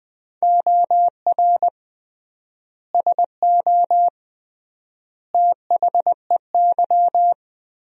Morse code, heard as a keyed tone, about 21 wpm, 700 Hz. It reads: OR SO THEY
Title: 3x Word-Spacing - Sentences from Top 100 Words - 20wpm